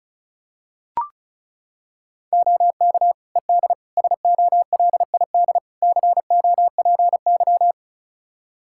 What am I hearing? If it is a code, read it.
Morse code, heard as a keyed tone, about 36 wpm, 700 Hz. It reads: OK ED SOLID COPY
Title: K1USN SST QSO Practice 35wpm